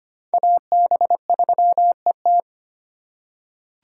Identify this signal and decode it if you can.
Morse code, heard as a keyed tone, about 25 wpm, 700 Hz. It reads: AB3ET
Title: USA Call Signs - 3x Character Spacing - 25wpm